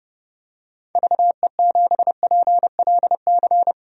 Morse code, heard as a keyed tone, about 30 wpm, 700 Hz. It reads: VE7PLC
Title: DXCC-100 Call Signs - 6 Characters - 30wpm